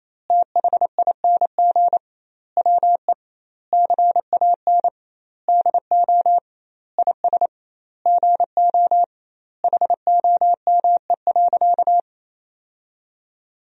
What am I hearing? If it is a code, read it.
Morse code, heard as a keyed tone, about 28 wpm, 700 Hz. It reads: THING WE CAN DO IS GO HOME.
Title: 2x Word-Spacing - Sentences from Top 200 Words - 28wpm